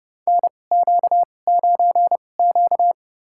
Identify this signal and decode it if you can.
Morse code, heard as a keyed tone, about 30 wpm, spaced as for 22 wpm, 700 Hz. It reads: NQ9Q
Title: USA Call Signs - 4 Characters - 2x Character Spacing - 30wpm